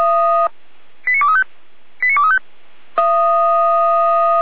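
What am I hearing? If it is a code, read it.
Five-tone selective call (ZVEI), heard as quick runs of five tones, tones of about 70 ms. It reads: 89235 89235
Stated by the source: ZVEI five-tone selective call received on an SDR.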